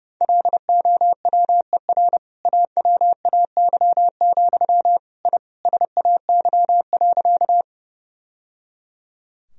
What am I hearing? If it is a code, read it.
Morse code, heard as a keyed tone, about 32 wpm, 700 Hz. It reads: LOWER AWAY, I SAY.
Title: Sentences from Top 1000 Words - Review Part 4 - 30wpm